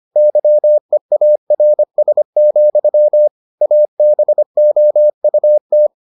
Morse code, heard as a keyed tone, about 25 wpm, 600 Hz. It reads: YEARS, ABOUT